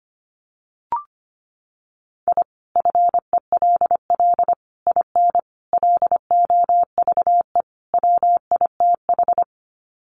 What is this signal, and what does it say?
Morse code, heard as a keyed tone, about 25 wpm, 700 Hz. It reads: I FELL IN LOVE WITH
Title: Sets of 5 Words Encore 25wpm